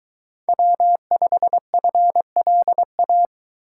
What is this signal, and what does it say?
Morse code, heard as a keyed tone, about 21 wpm, 700 Hz. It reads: W5FLA